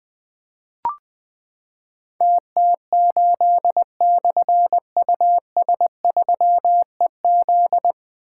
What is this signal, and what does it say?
Morse code, heard as a keyed tone, about 20 wpm, 700 Hz. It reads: TT8/US3EZ
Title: Difficult Call Signs 20wpm